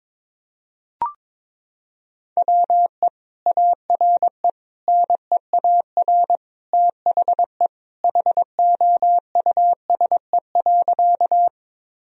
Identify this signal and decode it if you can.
Morse code, heard as a keyed tone, about 22 wpm, 700 Hz. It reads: WE ARE NEAR THE HOUSE.
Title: Sentences from Top 1000 Words - Review Part 1 - 22wpm